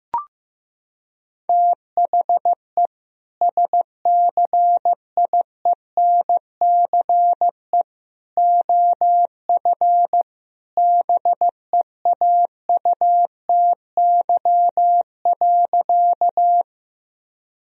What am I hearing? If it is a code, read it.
Morse code, heard as a keyed tone, about 16 wpm, 700 Hz. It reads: THE SCIENCE OF BEAUTY.
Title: Sentences from Top 1000 Words - Review Part 2 - 15wpm